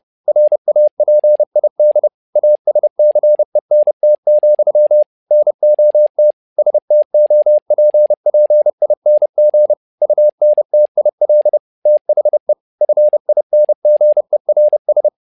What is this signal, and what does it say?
Morse code, heard as a keyed tone, about 30 wpm, 600 Hz. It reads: RAPID ASCENT, NOT STOPPING UNTIL THE FINGERS
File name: learning center / tower trouble / tower trouble 30wpm